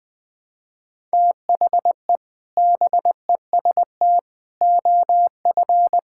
Morse code, heard as a keyed tone, about 21 wpm, 700 Hz. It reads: THE BEST OF
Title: Sentences with 2-Word Phrases - 20wpm